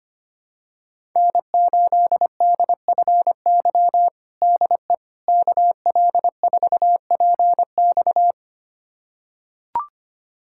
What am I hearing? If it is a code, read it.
Morse code, heard as a keyed tone, about 25 wpm, 700 Hz. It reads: N8DFY DE KL4PX
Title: USA Callsign DE Callsign - 5 Characters - 25wpm